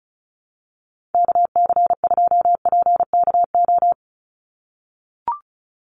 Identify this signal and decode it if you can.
Morse code, heard as a keyed tone, about 34 wpm, 700 Hz. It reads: KC2PKO
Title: USA Call Signs 35wpm